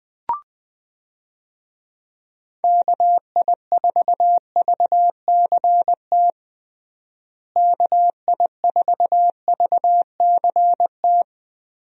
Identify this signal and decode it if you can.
Morse code, heard as a keyed tone, about 20 wpm, 700 Hz. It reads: KI4VCT KI4VCT
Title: USA Call Signs - 6 Characters - Repeated 3x - 20wpm